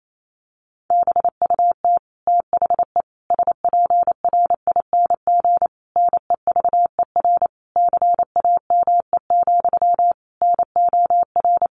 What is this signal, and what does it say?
Morse code, heard as a keyed tone, about 28 wpm, 700 Hz. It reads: BUT THE SPRING NEVER CAME, NOR
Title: Sentences from Top 1500 Words 28wpm